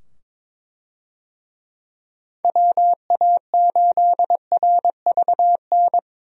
Morse code, heard as a keyed tone, about 21 wpm, 700 Hz. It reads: WA8RVN
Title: USA Call Signs - 6 Characters - 22wpm